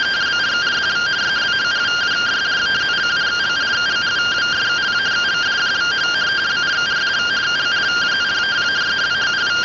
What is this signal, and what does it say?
Two-tone teleprinter signal, about 170 Hz shift.